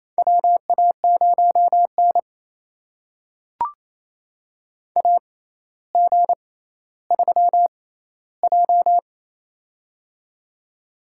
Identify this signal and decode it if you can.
Morse code, heard as a keyed tone, about 28 wpm, spaced as for 14 wpm, 700 Hz. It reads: WA0N AG3J
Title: USA Call Signs - 4 Characters - 6x Character Spacing - 28wpm